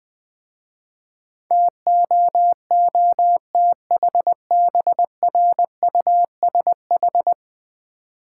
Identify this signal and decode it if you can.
Morse code, heard as a keyed tone, about 20 wpm, 700 Hz. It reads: TOOTHBRUSH
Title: Easy Words with Hints - 20wpm